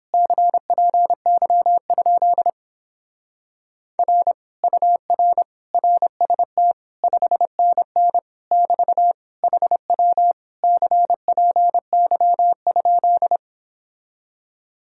Morse code, heard as a keyed tone, about 29 wpm, 700 Hz. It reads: CPY? R UR RST 5NN <BT> HW CPY?